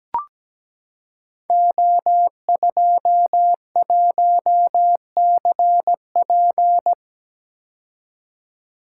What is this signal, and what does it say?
Morse code, heard as a keyed tone, about 17 wpm, 700 Hz. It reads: O21CP
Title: Code Groups - 5 Letter:Numbers - 17wpm